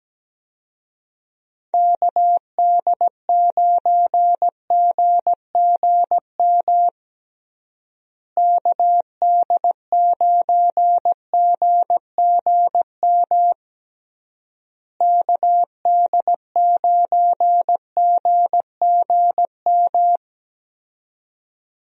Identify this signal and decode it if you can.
Morse code, heard as a keyed tone, about 17 wpm, 700 Hz. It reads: KD9GGM KD9GGM KD9GGM
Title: USA Call Signs - 6 Characters - Repeated 3x - 17wpm